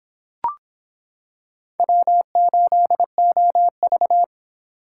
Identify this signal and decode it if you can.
Morse code, heard as a keyed tone, about 26 wpm, 700 Hz. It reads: W8OV